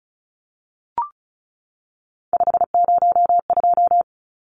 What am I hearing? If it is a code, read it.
Morse code, heard as a keyed tone, about 34 wpm, 700 Hz. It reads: 502
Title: Numbers - 3-Digit 35wpm